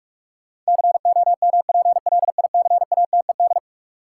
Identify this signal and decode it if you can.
Morse code, heard as a keyed tone, about 42 wpm, 700 Hz. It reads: COMPLICATED
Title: Sentences from Top 1200 Words - New Word Drill Alt - 45wpm